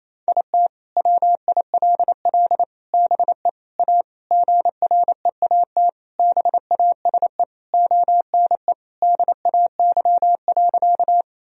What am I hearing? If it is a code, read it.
Morse code, heard as a keyed tone, about 28 wpm, 700 Hz. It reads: IT WILL BE A GREAT BASE ONE DAY.
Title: Sentences from Top 1000 Words - Review Part 1 - 28wpm